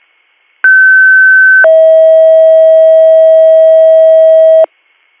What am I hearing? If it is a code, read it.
Two-tone page, heard as a one-second tone, then a three-second tone, about 1550 then 640 Hz. It is heard from 0.6 to 4.7 s.